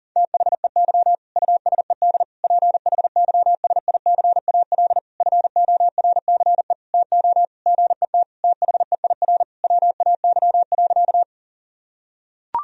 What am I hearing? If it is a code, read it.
Morse code, heard as a keyed tone, about 40 wpm, 700 Hz. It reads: THEY USED PHYSICAL FORCE TO GET THEIR WAY.
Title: Sentences from Top 1000 Words - Review Part 4 - 40wpm